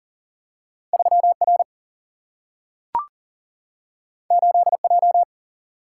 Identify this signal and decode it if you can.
Morse code, heard as a keyed tone, about 39 wpm, 700 Hz. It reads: 3R 8J
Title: Single-Letter Single-Number 40wpm